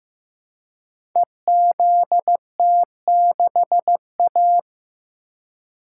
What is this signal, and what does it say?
Morse code, heard as a keyed tone, about 15 wpm, 700 Hz. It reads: EZT6A
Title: Code Groups - 5 Letter:Numbers - 15wpm